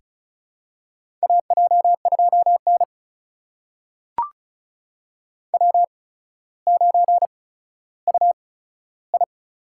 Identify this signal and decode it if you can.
Morse code, heard as a keyed tone, about 36 wpm, spaced as for 15 wpm, 700 Hz. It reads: AJ2N W9UI
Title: USA Call Signs - 4 Characters - 8x Character Spacing - 35wpm